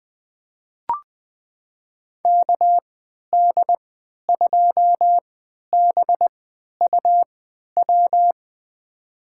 Morse code, heard as a keyed tone, about 21 wpm, spaced as for 11 wpm, 700 Hz. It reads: KD2BUW
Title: USA Call Signs - 6 Characters - 3x Character Spacing - 20wpm